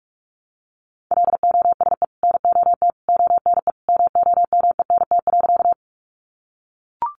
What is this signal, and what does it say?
Morse code, heard as a keyed tone, about 44 wpm, 700 Hz. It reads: LOSE NOT ONE MOMENT.
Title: Sentences from Top 1100 Words 45wpm